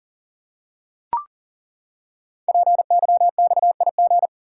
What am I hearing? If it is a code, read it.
Morse code, heard as a keyed tone, about 40 wpm, 700 Hz. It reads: PYXIG
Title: Code Groups - 5 Letters - 40wpm